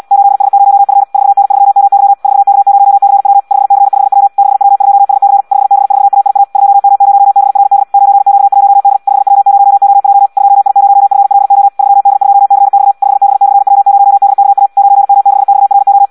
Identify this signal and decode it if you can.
Morse code, heard as a keyed tone, about 74 wpm, 800 Hz. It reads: 8S0F 4N3MY 4K0CW 5C5W 6W2SC 5B4AIA 2M0BRD 9Q1D 5W0OJ 2I0FLO 3D2CX 5B/W0DKA 9M6XRO
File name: Audio practica CW / qrq-tx-74 2